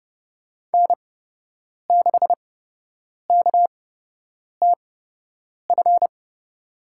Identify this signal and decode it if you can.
Morse code, heard as a keyed tone, about 32 wpm, spaced as for 8 wpm, 700 Hz. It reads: N6KTF
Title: USA Call Signs - 8x Character Spacing - 30wpm